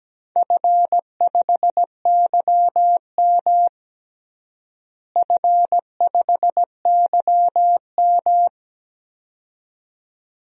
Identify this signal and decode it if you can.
Morse code, heard as a keyed tone, about 17 wpm, 700 Hz. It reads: F5YM F5YM
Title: DXCC-100 Call Signs - 4 Characters - Repeated 2x - 17wpm